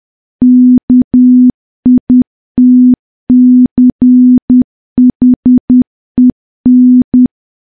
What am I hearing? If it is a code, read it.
Morse code, heard as a keyed tone, about 10 wpm, 250 Hz. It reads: KITCHEN
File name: morse / morse-training-10WPM